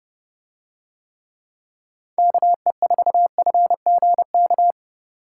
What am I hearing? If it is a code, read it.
Morse code, heard as a keyed tone, about 30 wpm, 700 Hz. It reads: KE4FGK